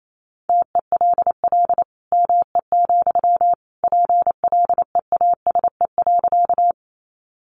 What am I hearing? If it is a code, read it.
Morse code, heard as a keyed tone, about 28 wpm, 700 Hz. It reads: TELL ME, PLEASE.
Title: Sentences from Top 1500 Words 28wpm